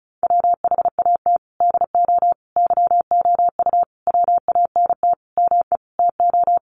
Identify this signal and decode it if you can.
Morse code, heard as a keyed tone, about 34 wpm, 700 Hz. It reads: WHAT DO YOU WANT ME TO
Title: Sentences from Top 200 Words 35wpm